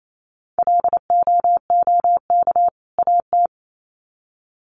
Morse code, heard as a keyed tone, about 28 wpm, 700 Hz. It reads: LOOK AT
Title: Sets of 2 Words 28wpm